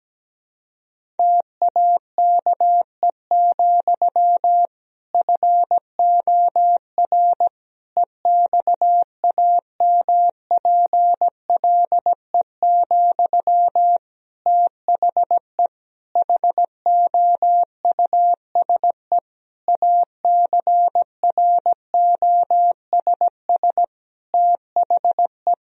Morse code, heard as a keyed tone, about 17 wpm, 700 Hz. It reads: TAKE, FOR EXAMPLE, THE HOUSE ACROSS THE
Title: Sentences from Top 1000 Words - Review Part 1 - 17wpm